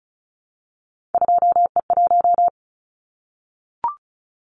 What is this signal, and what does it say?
Morse code, heard as a keyed tone, about 34 wpm, 700 Hz. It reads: 2E1
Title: Callsign Prefixes for Top 25 DXCC - Repeated 3x - 35wpm